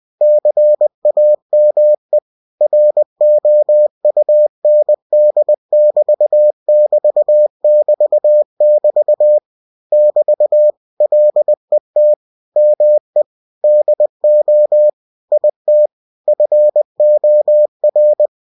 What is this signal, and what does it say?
Morse code, heard as a keyed tone, about 20 wpm, 600 Hz. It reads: CAME ROUND<BT><BT><BT><BT> <BT> LET ME DO IT FOR
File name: cw-books / winnie-the-pooh / winnie-20wpm-page-0048